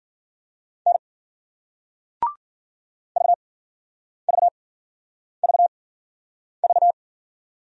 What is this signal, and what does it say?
Morse code, heard as a keyed tone, about 56 wpm, spaced as for 9 wpm, 700 Hz. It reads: N VVVV